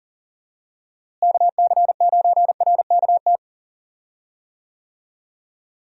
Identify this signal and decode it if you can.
Morse code, heard as a keyed tone, about 40 wpm, 700 Hz. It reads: KC9RKT
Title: USA Call Signs - 6 Characters - Repeated 2x - 40wpm